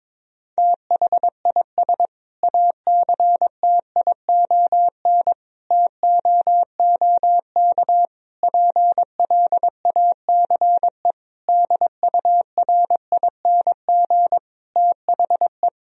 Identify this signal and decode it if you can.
Morse code, heard as a keyed tone, about 22 wpm, 700 Hz. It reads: THIS ACTION TOOK PLACE DURING THE
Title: Sentences from Top 1000 Words - Review Part 4 - 22wpm